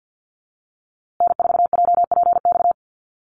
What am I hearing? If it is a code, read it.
Morse code, heard as a keyed tone, about 50 wpm, 700 Hz. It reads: N4JPX
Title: USA Call Signs Repeated 3x 50wpm